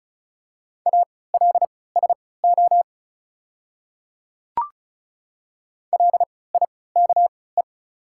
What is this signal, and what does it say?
Morse code, heard as a keyed tone, about 34 wpm, spaced as for 20 wpm, 700 Hz. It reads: ALSO LIKE
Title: 3x Character Spacing - Top 300 Words - 35wpm